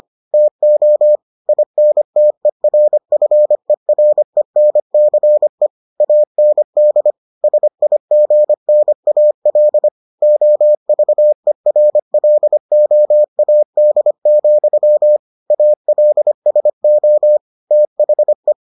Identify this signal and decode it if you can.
Morse code, heard as a keyed tone, about 25 wpm, 600 Hz. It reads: TO INTERFERENCE AND SIGNAL OVERLOAD, ALSO THE